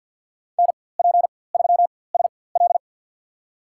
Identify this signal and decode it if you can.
Morse code, heard as a keyed tone, about 51 wpm, spaced as for 23 wpm, 700 Hz. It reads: NP3SL
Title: USA Call Signs - 5 Characters - 4x Character Spacing - 50wpm